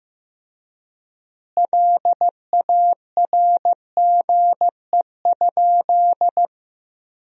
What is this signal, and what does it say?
Morse code, heard as a keyed tone, about 16 wpm, 700 Hz. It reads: LARGE?
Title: 3x Word-Spacing - Sentences from Top 200 Words - 15wpm